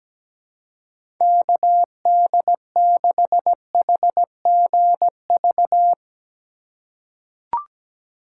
Morse code, heard as a keyed tone, about 17 wpm, 700 Hz. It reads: KD6HGV
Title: USA Call Signs - 6 Characters - 3x Character Spacing - 17wpm